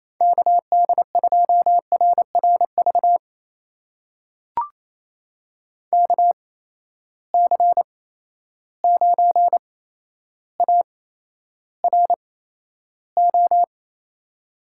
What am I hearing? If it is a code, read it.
Morse code, heard as a keyed tone, about 28 wpm, spaced as for 12 wpm, 700 Hz. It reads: KD2RRV KC9ARO
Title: USA Call Signs - 6 Characters - 8x Character Spacing - 28wpm